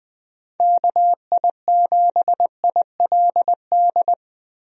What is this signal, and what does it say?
Morse code, heard as a keyed tone, about 20 wpm, 700 Hz. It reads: KI7ILD
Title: USA Call Signs - 6 Characters - Repeated 3x - 20wpm